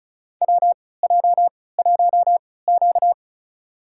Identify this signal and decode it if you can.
Morse code, heard as a keyed tone, about 36 wpm, spaced as for 20 wpm, 700 Hz. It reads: WJ1Q
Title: USA Call Signs - 4 Characters - 3x Character Spacing - 35wpm